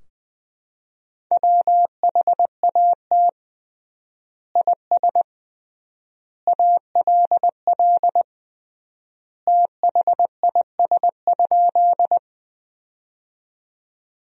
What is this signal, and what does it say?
Morse code, heard as a keyed tone, about 20 wpm, 700 Hz. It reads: WHAT IS ALL THIS?
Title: 3x Word-Spacing - Sentences from Top 100 Words - 20wpm